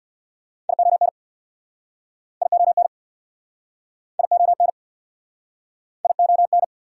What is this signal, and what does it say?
Morse code, heard as a keyed tone, about 69 wpm, 700 Hz. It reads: ION ION ION ION